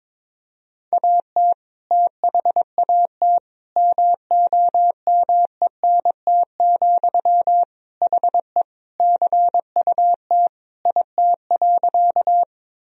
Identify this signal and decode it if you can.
Morse code, heard as a keyed tone, about 22 wpm, 700 Hz. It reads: AT THAT MOMENT, HE CUT IT.
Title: Sentences from Top 1000 Words - Review Part 1 - 22wpm